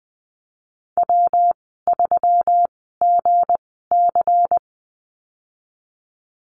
Morse code, heard as a keyed tone, about 19 wpm, spaced as for 14 wpm, 700 Hz. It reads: W3GC
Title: 173 Lesson 28g - Number 9 - Call Signs 20wpm